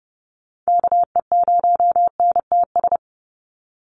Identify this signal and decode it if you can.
Morse code, heard as a keyed tone, about 30 wpm, 700 Hz. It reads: KE0NTS
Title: USA Call Signs 30wpm